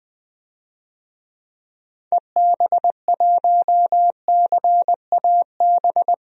Morse code, heard as a keyed tone, about 20 wpm, 700 Hz. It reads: EB1CAB
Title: DXCC-100 Call Signs - 6 Characters - 20wpm